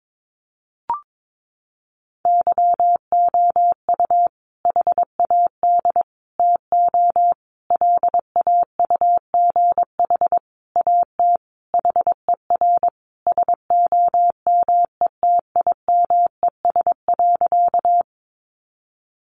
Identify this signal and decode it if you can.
Morse code, heard as a keyed tone, about 22 wpm, 700 Hz. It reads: YOU HAD TO LAUGH AT HER SOMETIMES.
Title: Sentences from Top 600 Words 22wpm